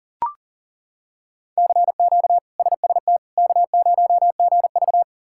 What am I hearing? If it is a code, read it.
Morse code, heard as a keyed tone, about 40 wpm, 700 Hz. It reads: CQ SST K0GV
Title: K1USN SST QSO Practice - 40wpm